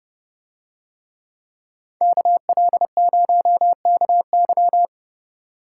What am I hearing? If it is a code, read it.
Morse code, heard as a keyed tone, about 29 wpm, 700 Hz. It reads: KL0KY